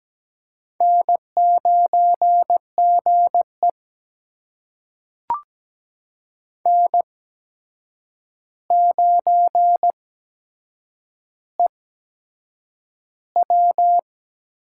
Morse code, heard as a keyed tone, about 17 wpm, spaced as for 7 wpm, 700 Hz. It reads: N9GE N9EW
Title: USA Call Signs - 4 Characters - 8x Character Spacing - 17wpm